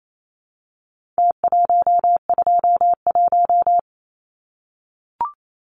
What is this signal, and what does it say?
Morse code, heard as a keyed tone, about 28 wpm, 700 Hz. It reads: T121
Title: Numbers - Contest Style - 4-Digit 28wpm